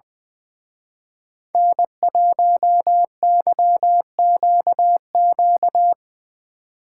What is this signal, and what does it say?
Morse code, heard as a keyed tone, about 22 wpm, 700 Hz. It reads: N1YQQ
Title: Speed Racing - USA 1x3 Call Signs 20wpm